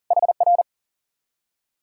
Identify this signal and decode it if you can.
Morse code, heard as a keyed tone, about 50 wpm, 700 Hz. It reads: HR
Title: Speed-Racing Common QSO Phrases - 40wpm